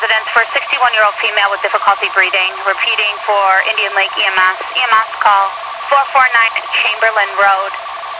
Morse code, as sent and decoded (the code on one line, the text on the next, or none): none